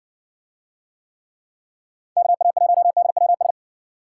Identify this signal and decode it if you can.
Morse code, heard as a keyed tone, about 60 wpm, 700 Hz. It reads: KA1DWS